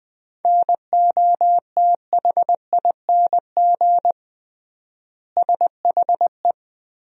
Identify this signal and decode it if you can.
Morse code, heard as a keyed tone, about 20 wpm, 700 Hz. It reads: NOTHING SHE
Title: 3x Word-Spacing - Sentences from Top 500 Words - 20wpm